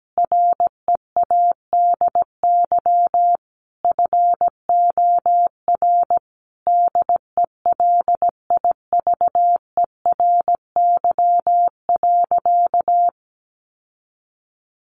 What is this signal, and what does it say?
Morse code, heard as a keyed tone, about 17 wpm, 700 Hz. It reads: READY FOR DELIVERY.
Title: Sentences from Top 1500 Words 17wpm